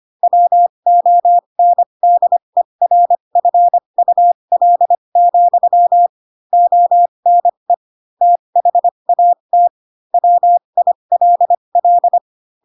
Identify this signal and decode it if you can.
Morse code, heard as a keyed tone, about 25 wpm, spaced as for 22 wpm, 700 Hz. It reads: WONDERFUL, ONE THAT WILL
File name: Pinocchio / Pinocchio ch2 18 25 ik1hge